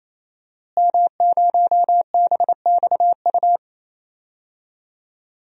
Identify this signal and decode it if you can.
Morse code, heard as a keyed tone, about 28 wpm, 700 Hz. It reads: M0BXU
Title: DXCC-100 Call Signs - 5 Characters - Repeated 3x - 28wpm